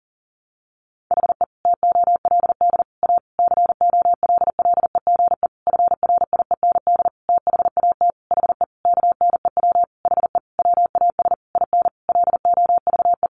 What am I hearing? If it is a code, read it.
Morse code, heard as a keyed tone, about 37 wpm, 700 Hz. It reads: HE TOLD A COLLEGE FRIEND THAT HE KNEW HE WAS IN LOVE
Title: Sentences from Top 1300 Words 40wpm